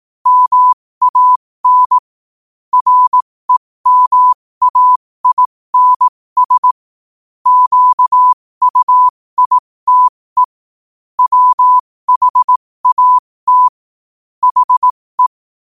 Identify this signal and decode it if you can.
Morse code, heard as a keyed tone, about 18 wpm, spaced as for 15 wpm, 1000 Hz. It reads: MAN REMAINS QUITE WHAT HE
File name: qotdfiles / 15wpm